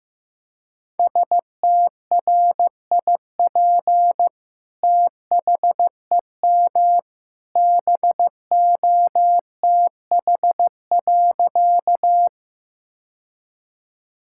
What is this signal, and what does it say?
Morse code, heard as a keyed tone, about 16 wpm, 700 Hz. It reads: STRIP THEM BOTH.
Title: Sentences from Top 1600 Words 15wpm